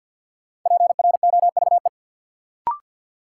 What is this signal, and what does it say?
Morse code, heard as a keyed tone, about 52 wpm, 700 Hz. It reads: PROVE
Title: Sentences from Top 1600 Words - New Word Drill Alt - 50wpm